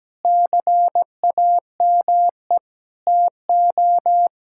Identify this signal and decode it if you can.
Morse code, heard as a keyed tone, about 17 wpm, 700 Hz. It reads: CAME TO
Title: Sentences from Top 1000 Words - Review Part 6 - 17wpm